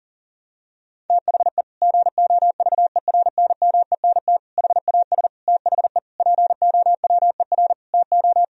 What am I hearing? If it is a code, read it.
Morse code, heard as a keyed tone, about 40 wpm, 700 Hz. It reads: THE GOVERNMENT HAS THE POWER TO
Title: Sentences from Top 1000 Words - Review Part 3 - 40wpm